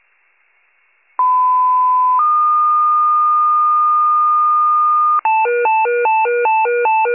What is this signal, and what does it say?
Two-tone page, heard as a one-second tone, then a three-second tone, about 1010 then 1220 Hz. It is heard from 1.2 to 5.2 s.
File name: pager / 94 RESCUE OFC / wcalert